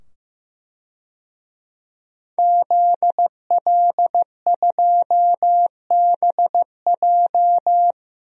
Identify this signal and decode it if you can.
Morse code, heard as a keyed tone, about 15 wpm, 700 Hz. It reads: ZL2BJ
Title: DXCC-100 Call Signs - 5 Characters - 15wpm